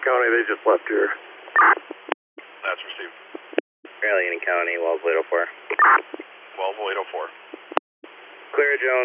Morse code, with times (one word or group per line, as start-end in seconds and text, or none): none